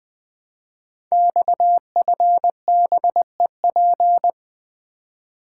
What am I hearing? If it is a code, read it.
Morse code, heard as a keyed tone, about 20 wpm, 700 Hz. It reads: XFBEP
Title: Code Groups - 5 Letter:Numbers - 20wpm